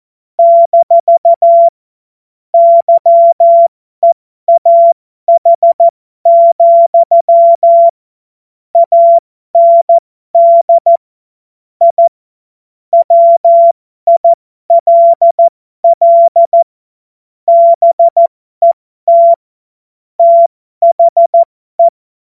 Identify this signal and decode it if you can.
Morse code, heard as a keyed tone, about 14 wpm, spaced as for 12 wpm, 680 Hz. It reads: - YEAH, AND I WILL BET THE